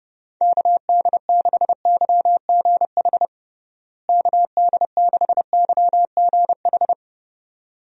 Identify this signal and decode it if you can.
Morse code, heard as a keyed tone, about 30 wpm, 700 Hz. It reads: KD6YGH KD6YGH
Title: USA Call Signs - 6 Characters - Repeated 2x - 30wpm